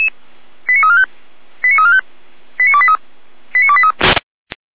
Five-tone selective call (ZVEI), heard as quick runs of five tones, tones of about 70 ms. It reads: 89345 89345 89283 89283
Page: /455/SDR/ZVEI/89171/2026.04.14/